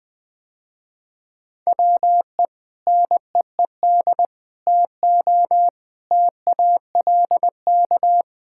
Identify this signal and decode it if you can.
Morse code, heard as a keyed tone, about 21 wpm, 700 Hz. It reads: WE NEED TO TALK
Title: Sentences with 4-Word Phrases - Spoken Beginning - 20wpm